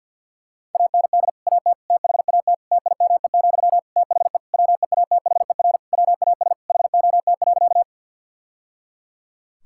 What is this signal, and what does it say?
Morse code, heard as a keyed tone, about 50 wpm, 700 Hz. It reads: AND AT THAT TIME, THE WEATHER WAS HOT.
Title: Sentences from Top 1000 Words - Review Part 5 - 50wpm